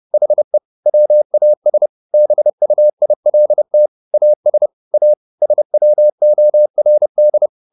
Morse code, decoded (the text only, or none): HE WAS BUILT AS A SWORD